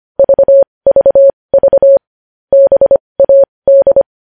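Morse code, heard as a keyed tone, about 25 wpm, spaced as for 20 wpm, 550 Hz. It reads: VVV BAD
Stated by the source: News headlines sent in Morse code at 20.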